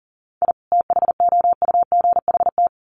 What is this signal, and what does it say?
Morse code, heard as a keyed tone, about 40 wpm, 700 Hz. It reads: I THOUGHT
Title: Sets of 2 Words 40wpm